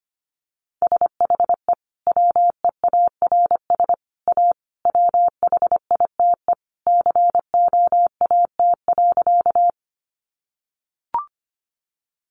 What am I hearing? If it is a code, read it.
Morse code, heard as a keyed tone, about 25 wpm, 700 Hz. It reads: SHE WEARS A WHITE COAT.
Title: Additional Easy Phrases 25wpm